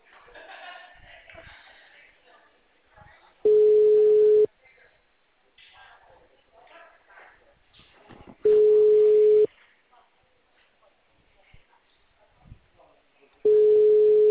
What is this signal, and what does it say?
Ringback tone (Europe, 425 Hz).
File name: out- -2015